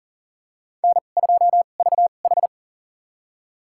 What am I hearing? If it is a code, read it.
Morse code, heard as a keyed tone, about 40 wpm, spaced as for 29 wpm, 700 Hz. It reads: N2VH
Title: USA Call Signs - 4 Characters - 2x Character Spacing - 40wpm